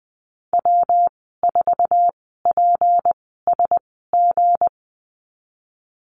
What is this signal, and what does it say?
Morse code, heard as a keyed tone, about 20 wpm, spaced as for 14 wpm, 700 Hz. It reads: W4PSG
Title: 193 Lesson 31g - Letter v - Call Signs 20wpm